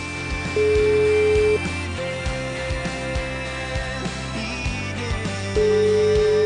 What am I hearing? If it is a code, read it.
Ringback tone (Europe, 425 Hz), with noise.